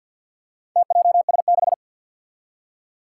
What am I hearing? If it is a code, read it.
Morse code, heard as a keyed tone, about 51 wpm, 700 Hz. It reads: TJS6
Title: Four Challenging Letter-Number Combinations 50wpm